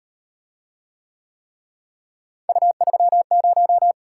Morse code, heard as a keyed tone, about 38 wpm, 700 Hz. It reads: U30